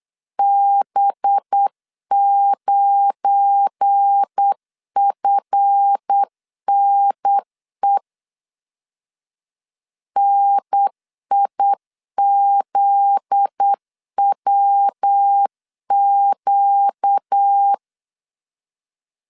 Morse code, heard as a keyed tone, about 8 wpm, 800 Hz. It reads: B9FNE NIZWQ
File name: LEZIONI / 8 YJ45 / 8 Y 8wpm 3245